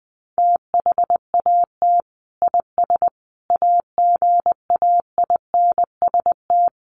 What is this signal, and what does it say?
Morse code, heard as a keyed tone, about 20 wpm, 700 Hz. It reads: THAT IS AGAINST
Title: Sentences from Top 1500 Words 20wpm